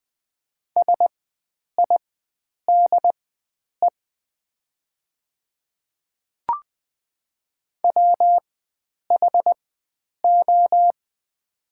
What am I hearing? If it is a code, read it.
Morse code, heard as a keyed tone, about 20 wpm, spaced as for 9 wpm, 700 Hz. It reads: SIDE WHO
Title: 4x Character Spacing - Top 100 Words - 20wpm